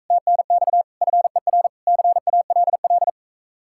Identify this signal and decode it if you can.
Morse code, heard as a keyed tone, about 42 wpm, 700 Hz. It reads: TNX FER CALL